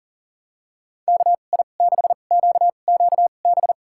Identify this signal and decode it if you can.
Morse code, heard as a keyed tone, about 40 wpm, spaced as for 29 wpm, 700 Hz. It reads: KI6QQB
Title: USA Call Signs - 6 Characters - 2x Character Spacing - 40wpm